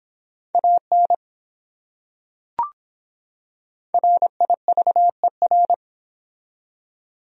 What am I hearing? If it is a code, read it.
Morse code, heard as a keyed tone, about 24 wpm, 700 Hz. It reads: AN RIVER